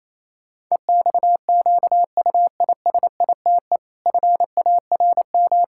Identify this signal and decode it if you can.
Morse code, heard as a keyed tone, about 28 wpm, 700 Hz. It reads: EXQUISITE FARM
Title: Pangram Challenge v2 - 28wpm